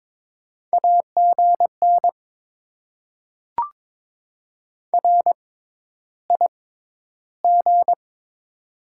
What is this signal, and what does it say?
Morse code, heard as a keyed tone, about 22 wpm, spaced as for 11 wpm, 690 Hz. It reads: AGN RIG
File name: Top 20 QSO Elements - 6x Character Spacing - 22wpm